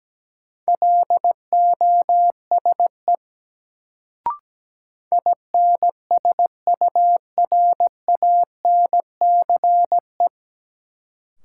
Morse code, heard as a keyed tone, about 17 wpm, 700 Hz. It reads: LOSE INSURANCE
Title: Sentences from Top 1100 Words - New Word Drill - 17wpm